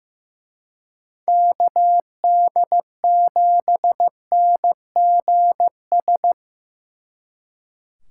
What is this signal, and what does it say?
Morse code, heard as a keyed tone, about 18 wpm, 700 Hz. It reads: KD7NGS
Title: Speed Racing - USA 2x3 Call Signs 15wpm